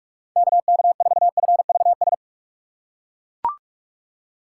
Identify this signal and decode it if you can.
Morse code, heard as a keyed tone, about 42 wpm, 700 Hz. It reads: KK4FVS
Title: USA Call Signs - 6 Characters - 4x Character Spacing - 45wpm